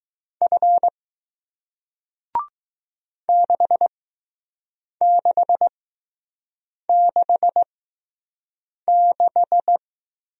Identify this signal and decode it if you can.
Morse code, heard as a keyed tone, about 20 wpm, 700 Hz. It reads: F 6 6 6 6